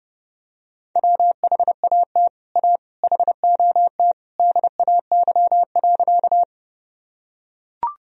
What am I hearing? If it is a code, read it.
Morse code, heard as a keyed tone, about 32 wpm, 700 Hz. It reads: WHAT A HOT DAY.
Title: Sentences from Top 1000 Words - Review Part 3 - 30wpm